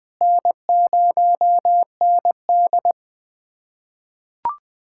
Morse code, heard as a keyed tone, about 20 wpm, 700 Hz. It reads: N0ND
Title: USA Callsign DE Callsign - 4 Characters - 20wpm